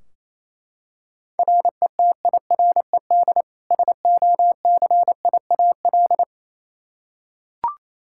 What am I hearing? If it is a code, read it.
Morse code, heard as a keyed tone, about 28 wpm, 700 Hz. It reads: RETIRED SOCIAL
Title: Practice QSO - Exchange 3 - More Info - 28wpm